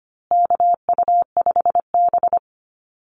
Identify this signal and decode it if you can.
Morse code, heard as a keyed tone, about 25 wpm, 700 Hz. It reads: KU5B
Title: Call Signs Repeated 2x - 25wpm